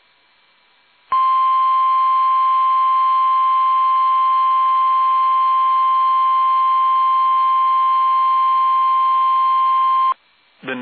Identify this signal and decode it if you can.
Emergency Alert System (SAME) tones (the 1050 Hz attention tone). It is heard from 1.1 to 10.1 s.